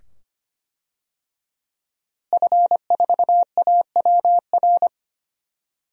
Morse code, heard as a keyed tone, about 25 wpm, 700 Hz. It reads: F4AWR
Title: DXCC-100 Call Signs - 5 Characters - Repeated 2x - 25wpm